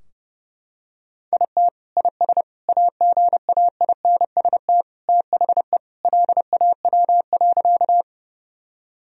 Morse code, heard as a keyed tone, about 31 wpm, 700 Hz. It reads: IT IS AGAINST THE LAW.
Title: Sentences from Top 1000 Words - Review Part 3 - 30wpm